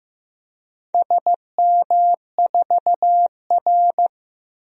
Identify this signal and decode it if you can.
Morse code, heard as a keyed tone, about 15 wpm, 700 Hz. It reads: SM4R
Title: DXCC-100 Call Signs - 4 Characters - Repeated 2x - 15wpm